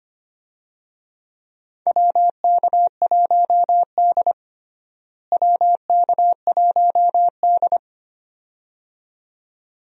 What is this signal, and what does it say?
Morse code, heard as a keyed tone, about 25 wpm, 700 Hz. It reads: WK1D WK1D
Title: USA Call Signs - 4 Characters - Repeated 2x - 25wpm